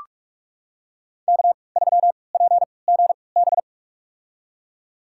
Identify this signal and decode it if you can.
Morse code, heard as a keyed tone, about 46 wpm, spaced as for 26 wpm, 700 Hz. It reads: K3PGB
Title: USA Call Signs - 5 Characters - 3x Character Spacing - 45wpm